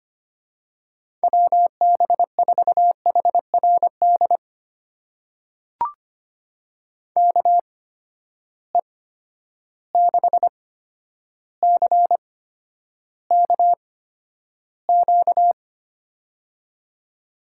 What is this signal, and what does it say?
Morse code, heard as a keyed tone, about 25 wpm, spaced as for 11 wpm, 700 Hz. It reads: WB4HRD KE6CKQ
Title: USA Call Signs - 6 Characters - 8x Character Spacing - 25wpm